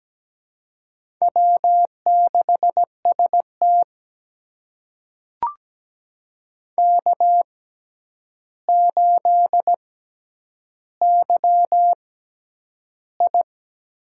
Morse code, heard as a keyed tone, about 17 wpm, spaced as for 9 wpm, 700 Hz. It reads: W6ST K8YI
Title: USA Call Signs - 6x Character Spacing - 17wpm